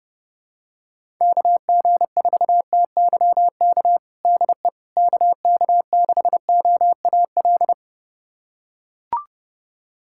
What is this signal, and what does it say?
Morse code, heard as a keyed tone, about 30 wpm, 700 Hz. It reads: KG4TYK DE KK6OAL
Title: USA Callsign DE Callsign - 6 Characters - 30wpm